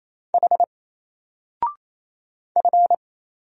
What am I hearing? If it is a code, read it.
Morse code, heard as a keyed tone, about 27 wpm, 700 Hz. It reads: H F
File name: Single-Letter-Number-28wpm